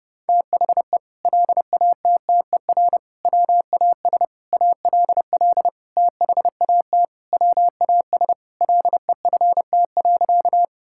Morse code, heard as a keyed tone, about 30 wpm, 700 Hz. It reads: THE LATTER WAS ALL THAT WAS LEFT.
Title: Sentences from Top 1600 Words 30wpm